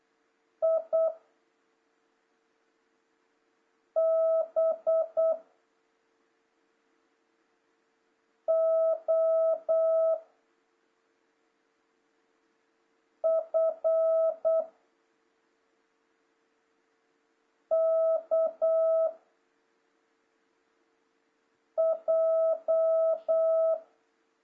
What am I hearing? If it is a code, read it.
Morse code, heard as a keyed tone, about 8 wpm, 660 Hz. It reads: I B O F K J
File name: jh8cbh / 2026cw